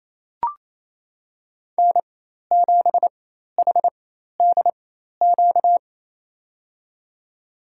Morse code, heard as a keyed tone, about 28 wpm, spaced as for 13 wpm, 700 Hz. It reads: N7HDQ
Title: USA Call Signs - 5 Characters - 4x Character Spacing - 28wpm